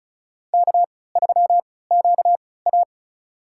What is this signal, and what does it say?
Morse code, heard as a keyed tone, about 36 wpm, spaced as for 20 wpm, 700 Hz. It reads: K3QA
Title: USA Call Signs - 3x Character Spacing - 35wpm